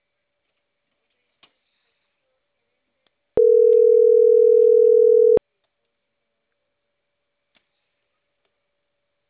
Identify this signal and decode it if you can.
Ringback tone (North America), 440 + 480 Hz.